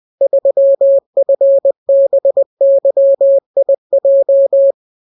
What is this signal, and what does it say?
Morse code, heard as a keyed tone, about 20 wpm, 550 Hz. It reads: 3FBYIJ